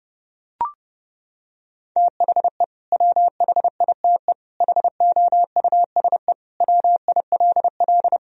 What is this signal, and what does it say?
Morse code, heard as a keyed tone, about 30 wpm, 700 Hz. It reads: THE WHITE HOUSE WILL
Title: Sentences from Top 1000 Words - Review Part 1 - 30wpm